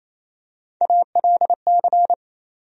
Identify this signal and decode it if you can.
Morse code, heard as a keyed tone, about 28 wpm, 700 Hz. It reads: ALC
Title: USA Call Sign Postfixes - 3 Letters 28wpm